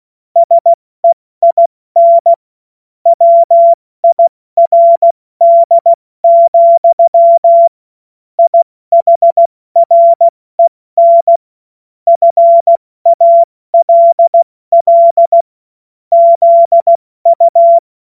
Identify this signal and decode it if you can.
Morse code, heard as a keyed tone, about 16 wpm, 680 Hz. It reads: SEIN WIRD, IHREN FALL ZU
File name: Morsekurs / Witze / Witz 10. Mai 2022 70 F80